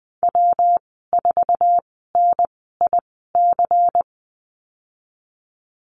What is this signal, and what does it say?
Morse code, heard as a keyed tone, about 20 wpm, spaced as for 14 wpm, 700 Hz. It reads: W4NIC
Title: 122 Lesson 20g - Number 3 - Call Signs 20wpm